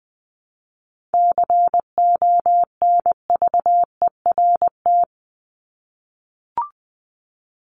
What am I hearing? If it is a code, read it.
Morse code, heard as a keyed tone, about 20 wpm, 700 Hz. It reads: CONVERT
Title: Top 1751-2000 Words 20wpm